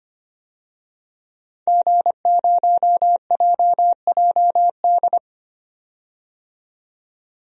Morse code, heard as a keyed tone, about 25 wpm, 700 Hz. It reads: G0JJD
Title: DXCC-100 Call Signs - 5 Characters - 25wpm